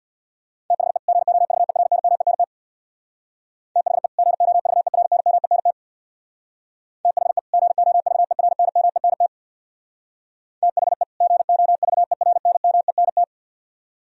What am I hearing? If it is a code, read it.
Morse code, heard as a keyed tone, about 60 wpm, 700 Hz. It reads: THE GOVERNMENT THE GOVERNMENT THE GOVERNMENT THE GOVERNMENT
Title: Speed-Racing - Sets of 2 Words 50wpm